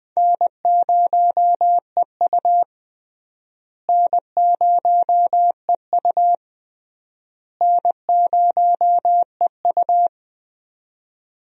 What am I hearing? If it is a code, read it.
Morse code, heard as a keyed tone, about 20 wpm, 700 Hz. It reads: N0EU N0EU N0EU
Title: USA Call Signs - 4 Characters - Repeated 3x - 20wpm